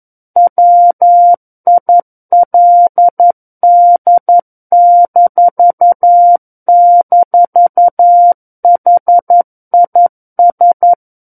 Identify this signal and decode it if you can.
Morse code, heard as a keyed tone, about 11 wpm, 700 Hz. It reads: WILD--HIS
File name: Book One / 02-Chapter 2 - 11 WPM